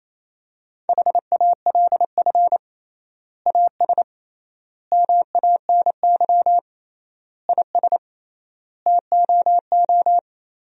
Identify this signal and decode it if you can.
Morse code, heard as a keyed tone, about 28 wpm, 700 Hz. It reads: HALF AS MANY IS TOO
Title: 3x Word-Spacing - Sentences from Top 400 Words - 28wpm